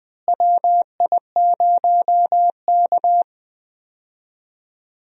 Morse code, heard as a keyed tone, about 20 wpm, 700 Hz. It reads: WI0K
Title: USA Call Signs - 4 Characters - Repeated 2x - 20wpm